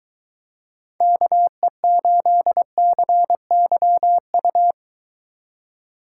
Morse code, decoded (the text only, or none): KE8CYU